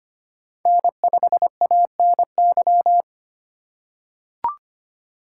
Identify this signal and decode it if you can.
Morse code, heard as a keyed tone, about 25 wpm, 700 Hz. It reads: N5ANY
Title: USA Call Signs - 5 Characters - 4x Character Spacing - 25wpm